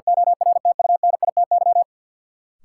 Morse code, heard as a keyed tone, about 53 wpm, 700 Hz. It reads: ORTUNITY
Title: Sentences from Top 1000 Words - New Word Drill - 50wpm